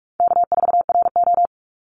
Morse code, heard as a keyed tone, about 44 wpm, 700 Hz. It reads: K4RO
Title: Call Signs - 45wpm